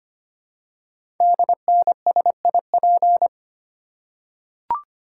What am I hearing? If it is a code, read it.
Morse code, heard as a keyed tone, about 25 wpm, 700 Hz. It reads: DNSIP
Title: Code Groups - 5 Letters - 25wpm